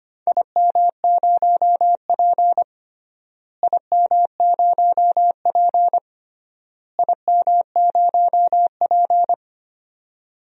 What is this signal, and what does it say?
Morse code, heard as a keyed tone, about 25 wpm, 700 Hz. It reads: IM0P IM0P IM0P
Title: DXCC-100 Call Signs - 4 Characters - Repeated 3x - 25wpm